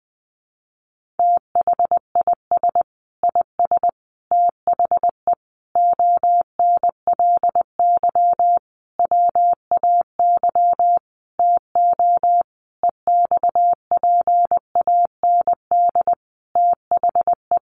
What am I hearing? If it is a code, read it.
Morse code, heard as a keyed tone, about 20 wpm, 700 Hz. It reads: THIS IS THE ONLY WAY TO EXPAND THE
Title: Sentences from Top 1400 Words 20wpm